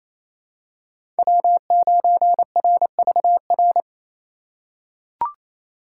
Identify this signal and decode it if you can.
Morse code, heard as a keyed tone, about 28 wpm, 700 Hz. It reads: W9RVR
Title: USA Call Signs - 3x Character Spacing - 28wpm